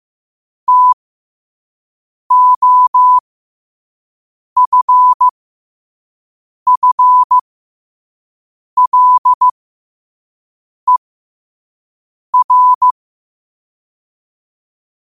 Morse code, heard as a keyed tone, about 15 wpm, spaced as for 5 wpm, 1000 Hz. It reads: TOFFLER
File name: qotdfiles / 5wpm